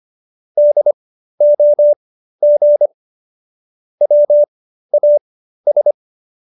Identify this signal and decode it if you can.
Morse code, heard as a keyed tone, about 25 wpm, spaced as for 13 wpm, 590 Hz. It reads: DOG WAS